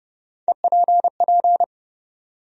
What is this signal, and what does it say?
Morse code, heard as a keyed tone, about 30 wpm, 700 Hz. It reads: EPP
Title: USA Call Sign Postfixes - 3 Letters 30wpm